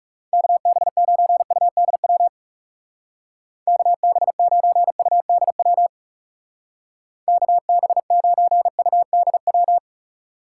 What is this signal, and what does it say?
Morse code, heard as a keyed tone, about 39 wpm, 700 Hz. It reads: KB9UDW KB9UDW KB9UDW